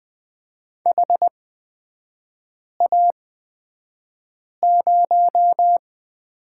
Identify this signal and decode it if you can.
Morse code, heard as a keyed tone, about 20 wpm, spaced as for 5 wpm, 700 Hz. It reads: HA0
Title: 243 Lesson 39e - Letter z - Three Characters 20wpm